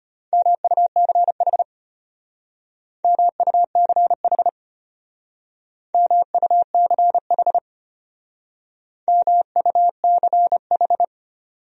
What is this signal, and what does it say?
Morse code, heard as a keyed tone, about 31 wpm, 700 Hz. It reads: MUCH MUCH MUCH MUCH